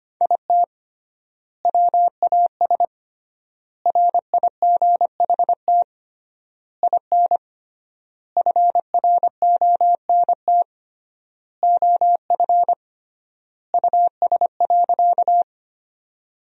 Morse code, heard as a keyed tone, about 25 wpm, 700 Hz. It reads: IT WAS RIGHT IN FRONT OF US.
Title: 3x Word-Spacing - Sentences from Top 500 Words - 25wpm